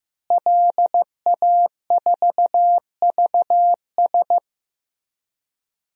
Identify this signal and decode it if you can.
Morse code, heard as a keyed tone, about 15 wpm, 700 Hz. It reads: LA4VS
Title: DXCC-100 Call Signs - 5 Characters - Repeated 2x - 15wpm